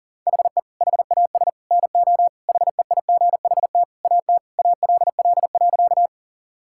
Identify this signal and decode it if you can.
Morse code, heard as a keyed tone, about 40 wpm, 700 Hz. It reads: HE HAS NO HEIGHT AT ALL.
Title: Sentences from Top 1600 Words 40wpm